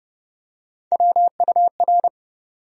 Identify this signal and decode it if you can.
Morse code, heard as a keyed tone, about 30 wpm, 700 Hz. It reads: WUR
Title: USA Call Sign Postfixes - 3 Letters 30wpm